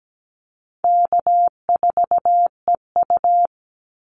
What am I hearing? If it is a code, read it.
Morse code, heard as a keyed tone, about 17 wpm, 700 Hz. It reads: K4EU
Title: Easy Call Signs Repeated 2x - 17wpm